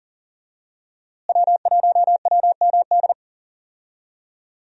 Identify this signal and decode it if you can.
Morse code, heard as a keyed tone, about 40 wpm, 700 Hz. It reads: W1WMD
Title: USA Call Signs - 5 Characters - Repeated 3x - 40wpm